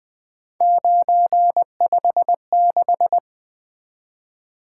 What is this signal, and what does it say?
Morse code, heard as a keyed tone, about 20 wpm, 700 Hz. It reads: 956
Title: Three Letter-Number Combinations - 20wpm